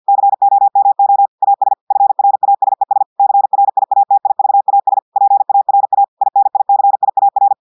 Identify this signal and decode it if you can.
Morse code, heard as a keyed tone, about 50 wpm, 800 Hz. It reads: COMO AS FRASES CRIATIVAS PARA INICIAR
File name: portugais texte en clair 2 vitesse 50 mots minutes